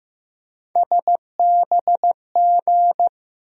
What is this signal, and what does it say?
Morse code, heard as a keyed tone, about 15 wpm, 700 Hz. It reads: SBG
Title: USA Call Sign Postfixes - 3 Letters 15wpm